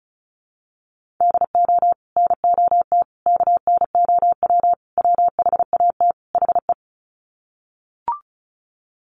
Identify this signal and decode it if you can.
Morse code, heard as a keyed tone, about 34 wpm, 700 Hz. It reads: DO NOT KNOW WHAT HE
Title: Sets of 5 Words Encore2 35wpm